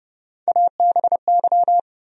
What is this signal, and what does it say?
Morse code, heard as a keyed tone, about 30 wpm, 700 Hz. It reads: ABY
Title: USA Call Sign Postfixes - 3 Letters 30wpm